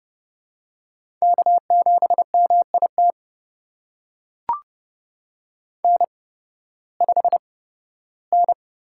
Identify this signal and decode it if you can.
Morse code, heard as a keyed tone, about 29 wpm, spaced as for 16 wpm, 700 Hz. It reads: K7MIT N5N
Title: USA Call Signs - 5 Characters - 8x Character Spacing - 30wpm